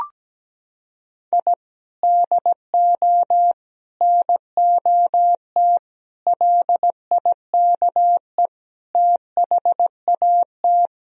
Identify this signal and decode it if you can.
Morse code, heard as a keyed tone, about 17 wpm, 700 Hz. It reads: I DO NOT LIKE THAT
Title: Sentences from Top 1000 Words - Review Part 2 - 17wpm